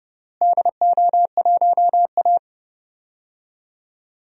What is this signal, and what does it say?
Morse code, heard as a keyed tone, about 30 wpm, 700 Hz. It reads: DO1A
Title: DXCC-100 Call Signs - 4 Characters - Repeated 3x - 30wpm